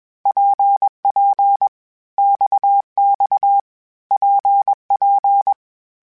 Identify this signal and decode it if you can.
Morse code, heard as a keyed tone, about 21 wpm, 800 Hz. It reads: PP XX PP
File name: Groups@20 / Groups@20 mp3s / RKPX@20